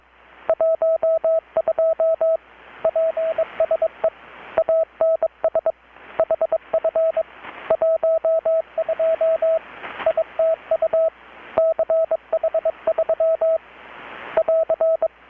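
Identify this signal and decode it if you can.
Morse code, heard as a keyed tone, about 22 wpm, 650 Hz, with radio noise. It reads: 12 PSE ANS HF 12 ITU CH3 <AR>
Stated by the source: Morse from KPH's Night of Nights XX maritime radiotelegraphy event, received in Chiapas, Mexico about 3,800 km away on a Kenwood R-600 with a grounded 10 m long-wire antenna.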